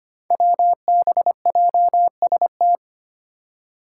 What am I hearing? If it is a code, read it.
Morse code, heard as a keyed tone, about 25 wpm, 700 Hz. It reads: WBJST
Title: Five Challenging Letter-Number Combinations 25wpm